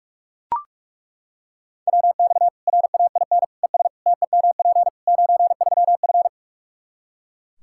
Morse code, heard as a keyed tone, about 42 wpm, 700 Hz. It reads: WX RAIN ES TEMP 93F
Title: Practice QSO - Exchange 2 - Info - 45wpm